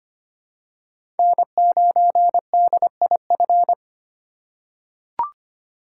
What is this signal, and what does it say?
Morse code, heard as a keyed tone, about 25 wpm, 700 Hz. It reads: N9DIF
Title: USA Call Signs - 5 Characters - 2x Character Spacing - 25wpm